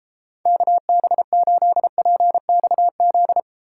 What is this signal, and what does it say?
Morse code, heard as a keyed tone, about 33 wpm, 700 Hz. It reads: KB8PXZ